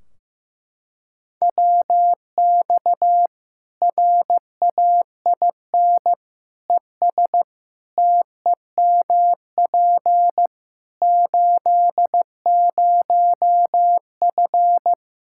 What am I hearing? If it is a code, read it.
Morse code, heard as a keyed tone, about 16 wpm, 700 Hz. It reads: WX RAIN ES TEMP 80F
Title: K1USN SST QSO Practice 15wpm